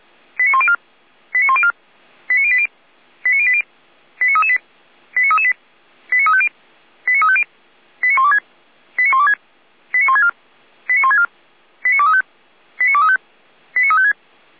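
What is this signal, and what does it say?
Five-tone selective call (ZVEI), heard as quick runs of five tones, tones of about 70 ms. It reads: 89104 89104 89080 89080 89338 89338 89350 89350 89126 89126 89164 89164 89235 89235 89356
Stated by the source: ZVEI five-tone call 89340 received by SDR.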